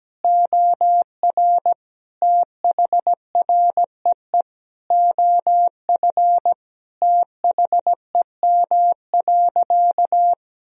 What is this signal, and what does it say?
Morse code, heard as a keyed tone, about 17 wpm, 700 Hz. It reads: OR THREE OF THEM.
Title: Sentences from Top 1000 Words - Review Part 1 - 17wpm